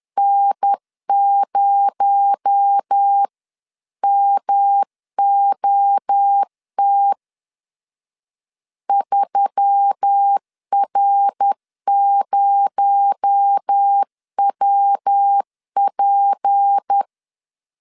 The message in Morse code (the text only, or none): N0MOT 3R0WP